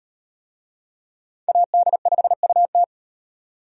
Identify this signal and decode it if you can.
Morse code, heard as a keyed tone, about 36 wpm, 700 Hz. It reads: AD5UT